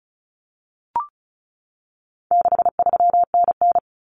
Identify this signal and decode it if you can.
Morse code, heard as a keyed tone, about 34 wpm, 700 Hz. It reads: 63NN
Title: Common Cut Numbers - 4-Digit 35wpm